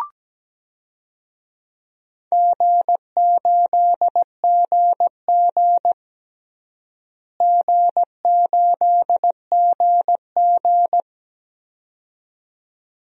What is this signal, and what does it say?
Morse code, heard as a keyed tone, about 17 wpm, 700 Hz. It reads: G8GG G8GG
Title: DXCC-100 Call Signs - 4 Characters - Repeated 2x - 17wpm